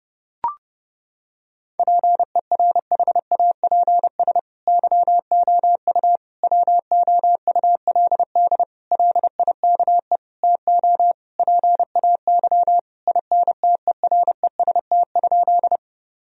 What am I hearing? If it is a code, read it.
Morse code, heard as a keyed tone, about 32 wpm, 700 Hz. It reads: PERHAPS YOU WOULD LIKE TO PAY INTEREST?
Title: Sentences from Top 1000 Words - Review Part 2 - 30wpm